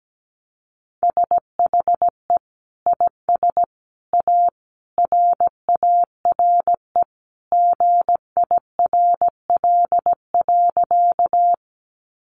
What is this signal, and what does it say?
Morse code, heard as a keyed tone, about 17 wpm, 700 Hz. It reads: SHE IS A RARE GIRL.
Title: Sentences from Top 1100 Words 17wpm